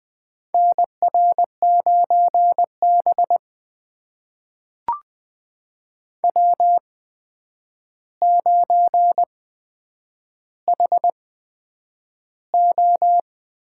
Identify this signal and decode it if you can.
Morse code, heard as a keyed tone, about 20 wpm, spaced as for 9 wpm, 700 Hz. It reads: NR9B W9HO
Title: USA Call Signs - 4 Characters - 8x Character Spacing - 20wpm